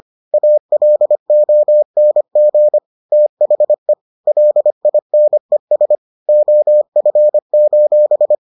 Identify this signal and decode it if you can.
Morse code, heard as a keyed tone, about 25 wpm, 600 Hz. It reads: ALONG THE LINES OF: